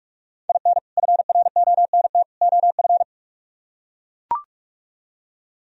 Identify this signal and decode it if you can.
Morse code, heard as a keyed tone, about 44 wpm, 700 Hz. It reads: IN FRONT OF